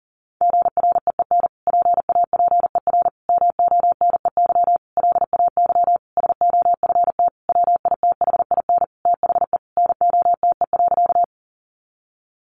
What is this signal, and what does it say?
Morse code, heard as a keyed tone, about 40 wpm, 700 Hz. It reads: GREEN PAPER MONEY LAY SOFT WITHIN THE NOTE.
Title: Sentences from Top 1500 Words 40wpm